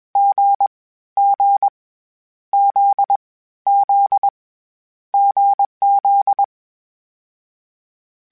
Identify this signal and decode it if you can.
Morse code, heard as a keyed tone, about 21 wpm, 800 Hz. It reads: G G Z Z GZ
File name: Difficult Combos / Difficult Combos-mp3s / awj1 ndb6 gz7 @20